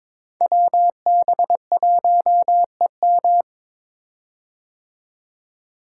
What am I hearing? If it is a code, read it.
Morse code, heard as a keyed tone, about 22 wpm, 700 Hz. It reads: WB1EM
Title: Code Groups - 5 Letter:Numbers - 22wpm